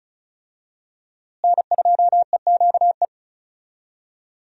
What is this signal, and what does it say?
Morse code, heard as a keyed tone, about 36 wpm, 700 Hz. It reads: N2EQE
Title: USA Call Signs - 5 Characters - Repeated 2x - 35wpm